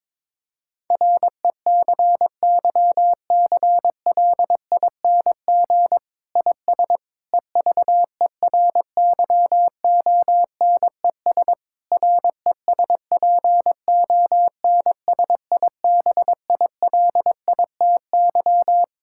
Morse code, heard as a keyed tone, about 22 wpm, 700 Hz. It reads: RECYCLING IS EVERYONES RESPONSIBILITY
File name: Difficult Sentences with Hints - 22wpm